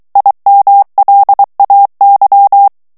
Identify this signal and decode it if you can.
Morse code, heard as a keyed tone, about 23 wpm, 800 Hz. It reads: IMLAY
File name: ANGLAIS texte clair 3 vitesse 22 mots minute